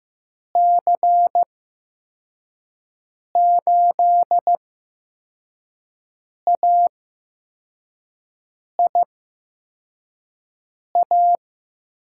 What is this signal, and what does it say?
Morse code, heard as a keyed tone, about 15 wpm, spaced as for 4 wpm, 700 Hz. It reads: C8AIA
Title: USA Call Signs - 8x Character Spacing - 15wpm